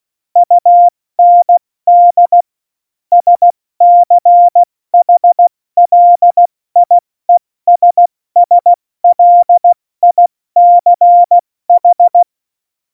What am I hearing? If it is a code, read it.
Morse code, heard as a keyed tone, about 16 wpm, 700 Hz. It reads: UND SCHLIESSLICH
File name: Morsekurs / Witze / Witz 26. Januar 2021, 70 F80